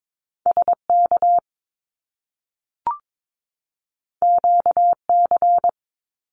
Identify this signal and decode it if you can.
Morse code, heard as a keyed tone, about 22 wpm, 700 Hz. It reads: SK QC
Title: Canadian Provinces and Territories 22wpm